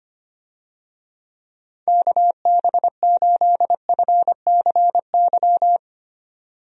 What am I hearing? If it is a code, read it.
Morse code, heard as a keyed tone, about 25 wpm, 700 Hz. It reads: KB8FCY
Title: USA Call Signs - 6 Characters - 25wpm